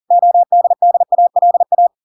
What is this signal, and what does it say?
Morse code, heard as a keyed tone, about 40 wpm, 700 Hz. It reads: ODDALA